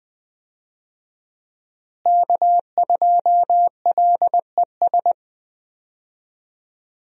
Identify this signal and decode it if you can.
Morse code, heard as a keyed tone, about 20 wpm, 700 Hz. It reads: K2LES
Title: USA Call Signs - 5 Characters - Repeated 2x - 20wpm